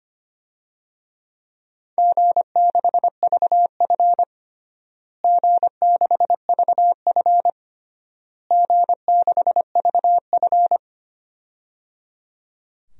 Morse code, heard as a keyed tone, about 25 wpm, 700 Hz. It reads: G6VF G6VF G6VF
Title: DXCC-100 Call Signs - 4 Characters - Repeated 3x - 25wpm